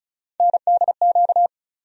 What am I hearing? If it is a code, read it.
Morse code, heard as a keyed tone, about 36 wpm, 700 Hz. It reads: NDQ
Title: USA Call Sign Postfixes - 3 Letters 35wpm